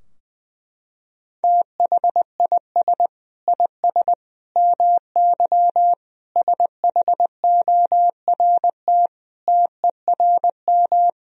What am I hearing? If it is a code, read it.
Morse code, heard as a keyed tone, about 20 wpm, 700 Hz. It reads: THIS IS MY SHORT TERM
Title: Sentences from Top 1000 Words - Review Part 4 - 20wpm